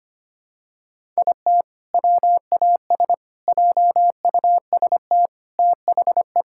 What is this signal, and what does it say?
Morse code, heard as a keyed tone, about 25 wpm, 700 Hz. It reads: IT WAS JUST THE
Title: Sentences from Top 1000 Words - Review Part 5 - 25wpm